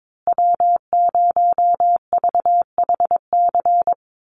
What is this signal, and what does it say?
Morse code, heard as a keyed tone, about 22 wpm, 700 Hz. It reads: W0VHC
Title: USA Call Signs Repeated 3x 22wpm